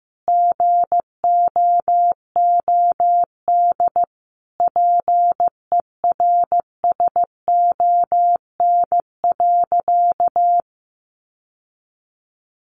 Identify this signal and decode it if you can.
Morse code, heard as a keyed tone, about 15 wpm, 700 Hz. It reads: GOOD PERSON.
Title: Sentences from Top 1300 Words 15wpm